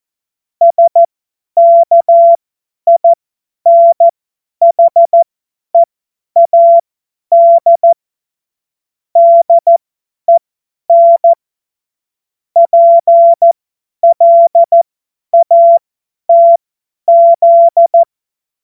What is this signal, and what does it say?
Morse code, heard as a keyed tone, about 14 wpm, spaced as for 10 wpm, 680 Hz. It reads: SKINHEAD DEN PLATZ
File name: Morsekurs / Witze / Witz 19. Oktober 2021, 50 F70